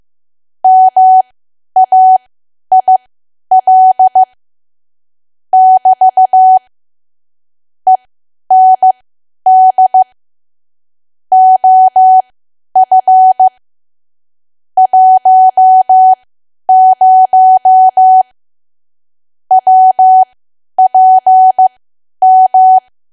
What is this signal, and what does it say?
Morse code, heard as a keyed tone, about 15 wpm, spaced as for 10 wpm, 750 Hz. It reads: MAIL <BT> END OF 10 WPM
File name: Archive / 10 WPM / 10WPM